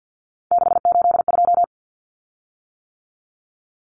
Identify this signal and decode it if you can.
Morse code, heard as a keyed tone, about 50 wpm, 700 Hz. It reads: 682
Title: Common Cut Numbers - 3-Digit 50wpm